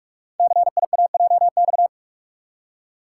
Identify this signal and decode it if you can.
Morse code, heard as a keyed tone, about 44 wpm, 700 Hz. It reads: CIAJX
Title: Code Groups - 5 Letters - 45wpm